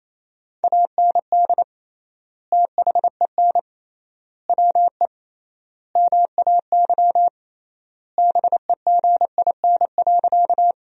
Morse code, heard as a keyed tone, about 28 wpm, 700 Hz. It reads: AND THEN WE MAY BEGIN.
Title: 3x Word-Spacing - Sentences from Top 300 Words - 28wpm